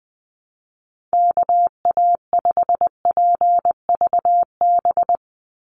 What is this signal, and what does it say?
Morse code, heard as a keyed tone, about 20 wpm, 700 Hz. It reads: KA5PVB
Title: NA SOTA Chaser Callsigns 20wpm